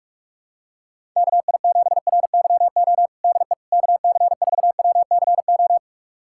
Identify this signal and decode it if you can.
Morse code, heard as a keyed tone, about 46 wpm, 700 Hz. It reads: KI7RYO DE KC4WCO
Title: USA Callsign DE Callsign - 6 Characters - 45wpm